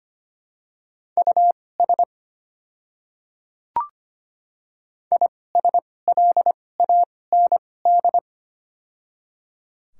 Morse code, heard as a keyed tone, about 25 wpm, spaced as for 18 wpm, 700 Hz. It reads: US ISLAND
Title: 2x Character Spacing - Top 500 Words - 25wpm